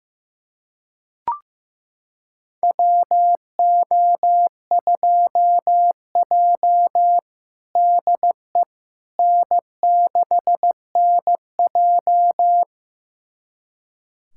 Morse code, heard as a keyed tone, about 15 wpm, 700 Hz. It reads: WO2J DE N6NJ
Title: USA Callsign DE Callsign - 4 Characters - 15wpm